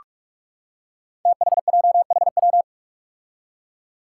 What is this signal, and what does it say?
Morse code, heard as a keyed tone, about 46 wpm, 700 Hz. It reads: THJHW
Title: Five Challenging Letter-Number Combinations 45wpm